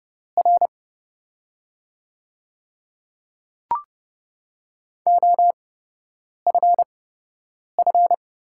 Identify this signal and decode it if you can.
Morse code, heard as a keyed tone, about 30 wpm, spaced as for 8 wpm, 700 Hz. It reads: R OFF
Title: 8x Character Spacing - Top 200 Words - 30wpm